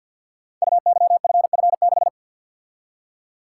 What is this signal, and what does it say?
Morse code, heard as a keyed tone, about 47 wpm, 700 Hz. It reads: UYFF6
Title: Code Groups - 5 Letter:Numbers - 50wpm